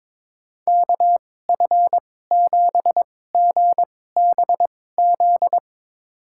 Morse code, heard as a keyed tone, about 22 wpm, spaced as for 16 wpm, 700 Hz. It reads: KF7GBZ
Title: USA Call Signs - 6 Characters - 2x Character Spacing - 22wpm